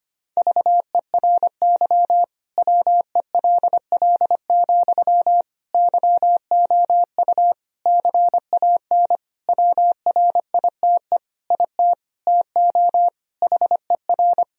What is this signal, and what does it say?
Morse code, heard as a keyed tone, about 25 wpm, 700 Hz. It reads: VERY WELL, YOU CAN WRITE IT TO HER
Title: Sentences from Top 1000 Words - Review Part 4 - 25wpm